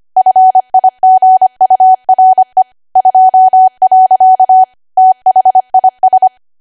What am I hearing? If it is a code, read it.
Morse code, heard as a keyed tone, about 25 wpm, 750 Hz. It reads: FIGURE 2. THIS
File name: Archive / 25 WPM / 25WPM